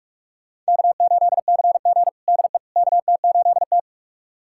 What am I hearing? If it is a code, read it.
Morse code, heard as a keyed tone, about 41 wpm, 700 Hz. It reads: K8CG DE KT8T
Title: USA Callsign DE Callsign - 4 Characters - 45wpm